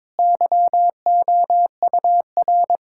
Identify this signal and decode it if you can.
Morse code, heard as a keyed tone, about 22 wpm, 700 Hz. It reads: YOUR
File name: Difficult Sentences with Hints - 22wpm